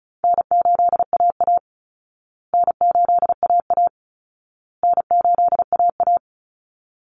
Morse code, heard as a keyed tone, about 34 wpm, 700 Hz. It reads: N8AA N8AA N8AA
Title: Call Signs Repeated 2x - 35wpm